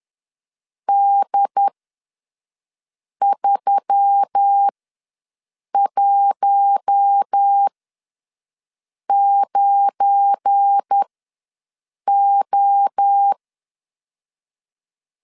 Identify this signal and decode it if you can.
Morse code, heard as a keyed tone, about 10 wpm, spaced as for 5 wpm, 800 Hz. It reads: D319O
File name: LEZIONI / 8 YJ45 / 8 YJ ws 10wpm 4043